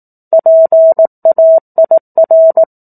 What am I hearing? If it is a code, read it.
Morse code, heard as a keyed tone, about 18 wpm, 650 Hz. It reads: PAIR
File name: 04-Chapter-04 18 WPM